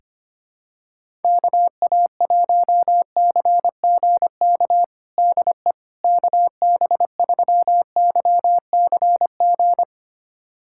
Morse code, heard as a keyed tone, about 25 wpm, 700 Hz. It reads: KA1CGK DE KB3YCG
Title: USA Callsign DE Callsign - 6 Characters - 25wpm